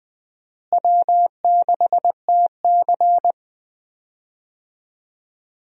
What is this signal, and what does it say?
Morse code, heard as a keyed tone, about 21 wpm, 700 Hz. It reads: W6TC
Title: USA Call Signs - 4 Characters - Repeated 2x - 20wpm